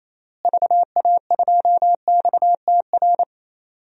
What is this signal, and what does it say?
Morse code, heard as a keyed tone, about 28 wpm, 700 Hz. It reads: VA2XTR
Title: DXCC-100 Call Signs - 6 Characters - Repeated 3x - 28wpm